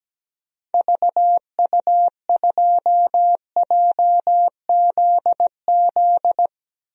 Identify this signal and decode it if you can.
Morse code, heard as a keyed tone, about 17 wpm, 700 Hz. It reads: VU2JZZ
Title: DXCC-100 Call Signs - 6 Characters - 17wpm